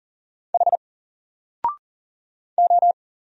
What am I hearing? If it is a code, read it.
Morse code, heard as a keyed tone, about 40 wpm, 700 Hz. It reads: H O
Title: Single Character (Letter:Number:Punctuation:Prosign) Alt 40wpm